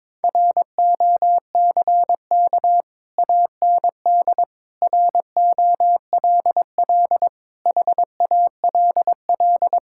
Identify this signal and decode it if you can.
Morse code, heard as a keyed tone, about 22 wpm, 700 Hz. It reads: ROCK AND ROLL HALL
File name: practice / 22wpm / Sets of 6 Words - 22wpm